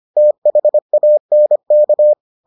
Morse code, heard as a keyed tone, about 25 wpm, 600 Hz. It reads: THANK